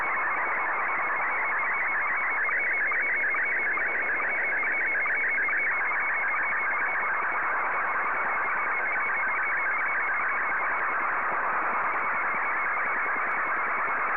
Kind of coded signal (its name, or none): SSTV (slow-scan television)